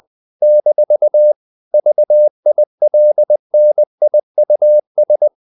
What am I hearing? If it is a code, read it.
Morse code, heard as a keyed tone, about 20 wpm, 600 Hz. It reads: - VILNIUS
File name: CW / Practice / NationalCapitals20